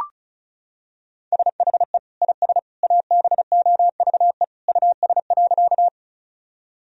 Morse code, heard as a keyed tone, about 36 wpm, 700 Hz. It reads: SHE IS ABOVE US.
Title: Sentences from Top 1000 Words - Review Part 2 - 35wpm